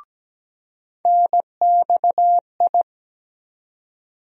Morse code, heard as a keyed tone, about 17 wpm, 700 Hz. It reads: NXI
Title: USA Call Sign Postfixes - 3 Letters 17wpm